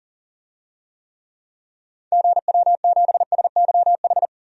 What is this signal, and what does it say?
Morse code, heard as a keyed tone, about 40 wpm, 700 Hz. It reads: GW7SYH
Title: DXCC-100 Call Signs - 6 Characters - 40wpm